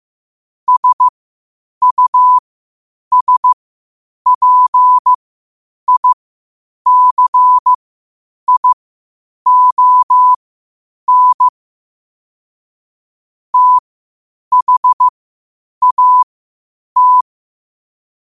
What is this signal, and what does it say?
Morse code, heard as a keyed tone, about 15 wpm, spaced as for 8 wpm, 1000 Hz. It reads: SUSPICION THAT
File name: qotdfiles / 8wpm